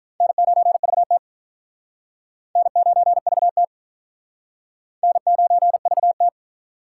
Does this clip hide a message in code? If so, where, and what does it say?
Yes: Morse code from 0.2 to 6.3 s, heard as a keyed tone, about 45 wpm, 700 Hz. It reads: N9VT N9VT N9VT